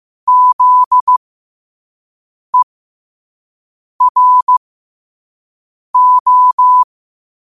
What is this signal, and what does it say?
Morse code, heard as a keyed tone, about 15 wpm, spaced as for 5 wpm, 1000 Hz. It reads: ZERO
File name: qotdfiles / 5wpm